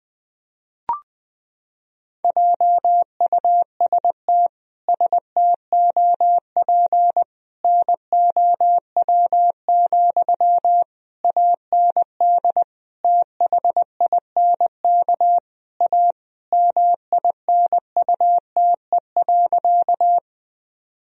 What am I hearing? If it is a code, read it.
Morse code, heard as a keyed tone, about 20 wpm, 700 Hz. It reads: JUST STOP NOW, AND THINK A MINUTE.
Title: Sentences from Top 1000 Words - Review Part 1 - 20wpm